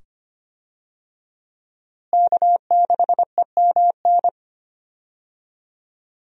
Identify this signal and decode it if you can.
Morse code, heard as a keyed tone, about 25 wpm, 700 Hz. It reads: K6EMN
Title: USA Call Signs - 5 Characters - Repeated 2x - 25wpm